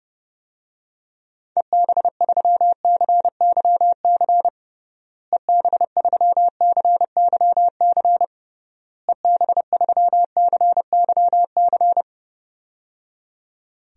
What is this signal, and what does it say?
Morse code, heard as a keyed tone, about 30 wpm, 700 Hz. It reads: EB3CYC EB3CYC EB3CYC
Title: DXCC-100 Call Signs - 6 Characters - Repeated 3x - 30wpm